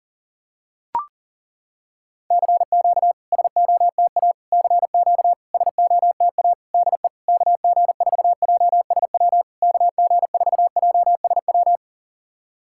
Morse code, heard as a keyed tone, about 40 wpm, 700 Hz. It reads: CQ SOTA CQ SOTA DE KG4JSW KG4JSW
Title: SOTA QSO Practice 40wpm